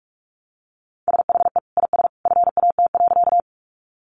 Morse code, heard as a keyed tone, about 44 wpm, 700 Hz. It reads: SHE IS FAT.
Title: Sentences from Top 600 Words 45wpm